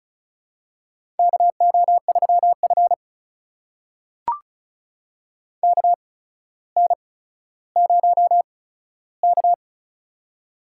Morse code, heard as a keyed tone, about 36 wpm, spaced as for 15 wpm, 700 Hz. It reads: KO3F KN0K
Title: USA Call Signs - 4 Characters - 8x Character Spacing - 35wpm